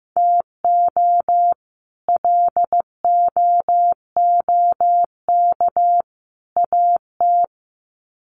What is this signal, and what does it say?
Morse code, heard as a keyed tone, about 15 wpm, 700 Hz. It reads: TO LOOK AT
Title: Sets of 3 Words 15wpm